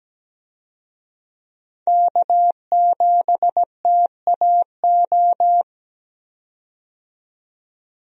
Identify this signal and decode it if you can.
Morse code, heard as a keyed tone, about 17 wpm, 700 Hz. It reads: K7TAO
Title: USA Call Signs - 5 Characters - 17wpm